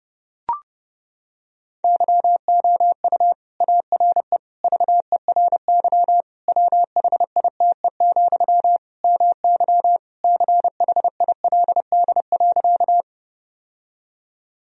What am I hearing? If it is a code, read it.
Morse code, heard as a keyed tone, about 30 wpm, 700 Hz. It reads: YOU ARE VERY WHITE, MY CHILD.
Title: Sentences from Top 1000 Words - Review Part 1 - 30wpm